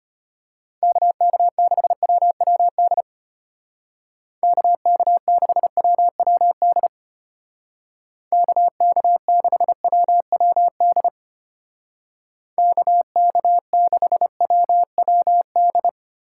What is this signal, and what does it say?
Morse code, heard as a keyed tone, about 33 wpm, 700 Hz. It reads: KK6WWD KK6WWD KK6WWD KK6WWD